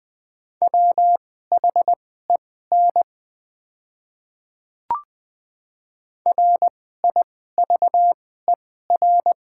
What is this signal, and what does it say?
Morse code, heard as a keyed tone, about 20 wpm, spaced as for 15 wpm, 690 Hz. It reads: WHEN RIVER
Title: 2x Character Spacing - Top 500 Words - 20wpm